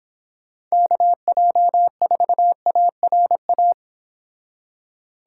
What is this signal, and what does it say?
Morse code, heard as a keyed tone, about 26 wpm, 700 Hz. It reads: KJ4ARA